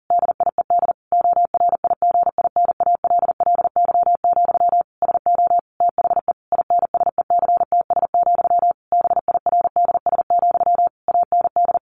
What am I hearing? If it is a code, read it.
Morse code, heard as a keyed tone, about 40 wpm, 700 Hz. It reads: DIED ORIGINALLY, SO THE INSECTS, BIRDS, AND
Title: A Sound Of Thunder 40wpm